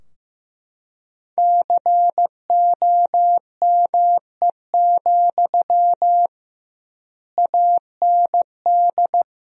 Morse code, heard as a keyed tone, about 15 wpm, 700 Hz. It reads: COME, AND
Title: 2x Word-Spacing - Sentences from Top 500 Words - 15wpm